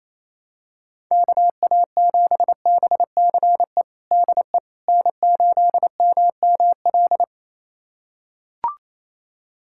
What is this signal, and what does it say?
Morse code, heard as a keyed tone, about 27 wpm, 700 Hz. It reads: KA7BCE DE N8MML
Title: USA Callsign DE Callsign 28wpm